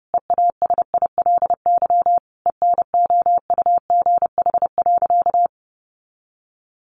Morse code, heard as a keyed tone, about 30 wpm, 700 Hz. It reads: EASILY ENOUGH.
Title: Sentences from Top 900 Words 30wpm